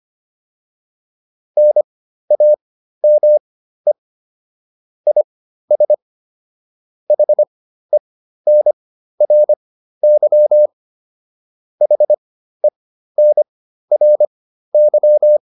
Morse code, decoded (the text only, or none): NAME IS HENRY HENRY